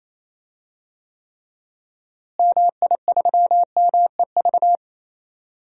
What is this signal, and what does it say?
Morse code, heard as a keyed tone, about 27 wpm, 700 Hz. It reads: MI3MEV
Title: DXCC-100 Call Signs - 6 Characters - Repeated 2x - 28wpm